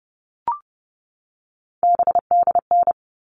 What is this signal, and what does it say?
Morse code, heard as a keyed tone, about 30 wpm, 700 Hz. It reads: BDN
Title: All Cut Numbers - 3-Digit 30wpm